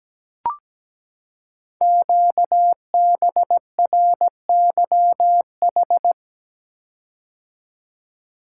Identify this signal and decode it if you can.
Morse code, heard as a keyed tone, about 17 wpm, 700 Hz. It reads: QBRYH
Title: Code Groups - 5 Letters - 17wpm